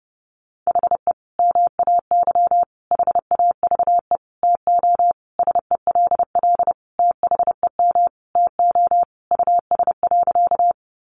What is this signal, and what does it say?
Morse code, heard as a keyed tone, about 30 wpm, 700 Hz. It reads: HE MAY HAVE TO SELL THEM TO US.
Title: Sentences from Top 1100 Words 30wpm